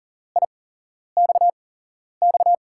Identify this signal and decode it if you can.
Morse code, heard as a keyed tone, about 39 wpm, 700 Hz. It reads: I X X
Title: USA Call Signs - 6 Characters - 8x Character Spacing - 40wpm